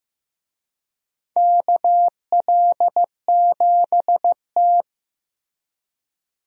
Morse code, heard as a keyed tone, about 15 wpm, 700 Hz. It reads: KL7T
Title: DXCC-100 Call Signs - 4 Characters - Repeated 2x - 15wpm